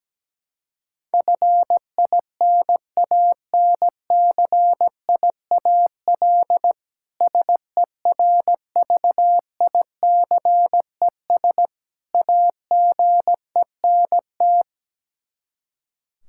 Morse code, heard as a keyed tone, about 17 wpm, 700 Hz. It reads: FINANCIAL SERVICES AGENT
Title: K1USN SST QSO Practice 17wpm